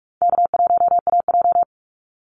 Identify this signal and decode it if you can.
Morse code, heard as a keyed tone, about 44 wpm, 700 Hz. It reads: K1AJ
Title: Call Signs Repeated 3x - 45wpm